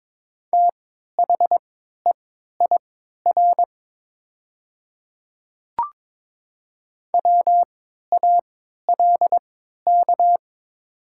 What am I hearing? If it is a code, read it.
Morse code, heard as a keyed tone, about 22 wpm, spaced as for 13 wpm, 700 Hz. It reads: THEIR WALK
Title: 3x Character Spacing - Top 300 Words - 22wpm